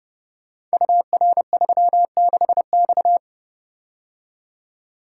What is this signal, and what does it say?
Morse code, heard as a keyed tone, about 30 wpm, 700 Hz. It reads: UR36X
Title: Code Groups - 5 Letter:Numbers - 30wpm